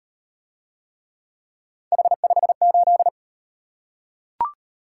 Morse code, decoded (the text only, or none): H58